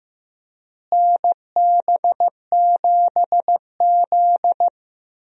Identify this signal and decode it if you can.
Morse code, heard as a keyed tone, about 15 wpm, 700 Hz. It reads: NB7Z
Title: USA Call Signs - 4 Characters - Repeated 3x - 15wpm